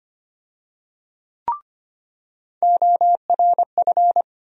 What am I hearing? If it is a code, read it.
Morse code, heard as a keyed tone, about 25 wpm, 700 Hz. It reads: ORF
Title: USA Call Sign Postfixes - 3 Letters 25wpm